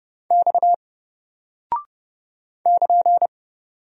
Morse code, heard as a keyed tone, about 30 wpm, 700 Hz. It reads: X <KN>
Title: Single Character (Letter:Number:Punctuation:Prosign) 30wpm